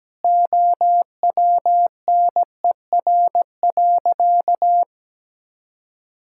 Morse code, heard as a keyed tone, about 17 wpm, 700 Hz. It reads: OWNER.
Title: Sentences from Top 1600 Words 17wpm